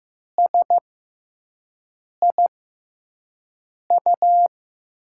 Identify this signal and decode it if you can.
Morse code, heard as a keyed tone, about 15 wpm, spaced as for 5 wpm, 700 Hz. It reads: SIU
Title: USA Call Signs - 5 Characters - 6x Character Spacing - 15wpm